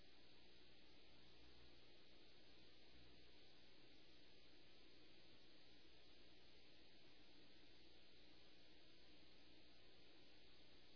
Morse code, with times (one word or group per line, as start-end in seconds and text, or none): none